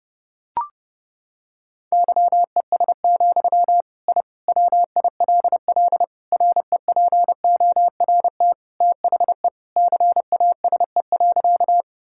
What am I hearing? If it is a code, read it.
Morse code, heard as a keyed tone, about 32 wpm, 700 Hz. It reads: YES, I WILL REPORT THE CASE.
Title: Sentences from Top 1000 Words - Review Part 6 - 30wpm